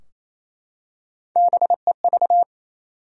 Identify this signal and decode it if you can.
Morse code, heard as a keyed tone, about 27 wpm, 700 Hz. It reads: BEV
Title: USA Common Names - Max 3 Letters - 28wpm